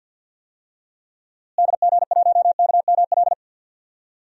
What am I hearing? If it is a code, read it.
Morse code, heard as a keyed tone, about 51 wpm, 700 Hz. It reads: DG1KML
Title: DXCC-100 Call Signs - 6 Characters - Repeated 3x - 50wpm